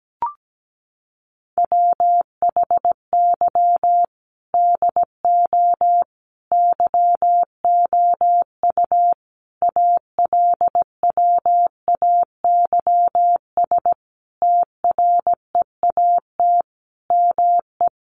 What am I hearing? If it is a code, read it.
Morse code, heard as a keyed tone, about 17 wpm, 700 Hz. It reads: WHY DO YOU ALWAYS TREAT ME
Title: Sentences from Top 1300 Words 17wpm